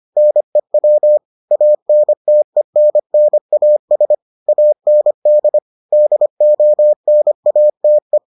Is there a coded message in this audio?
Yes: Morse code from 0.2 to 8.2 s, heard as a keyed tone, about 25 wpm, 600 Hz. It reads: NEW ANTENNAS AND DONATE